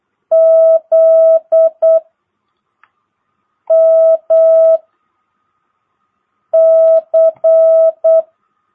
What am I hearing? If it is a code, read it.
Morse code, heard as a keyed tone, about 8 wpm, 660 Hz. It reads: Z M C
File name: jh8cbh / 2022cw